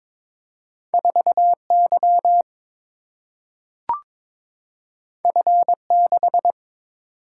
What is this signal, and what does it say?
Morse code, heard as a keyed tone, about 22 wpm, 700 Hz. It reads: 4Y F6
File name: practice / 22wpm / Two Letter-Number Combinations - 22wpm